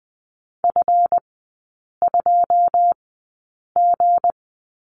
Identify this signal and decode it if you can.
Morse code, heard as a keyed tone, about 20 wpm, spaced as for 8 wpm, 700 Hz. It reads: F2G
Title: 159 Lesson 26e - Letter g - Three Characters 20wpm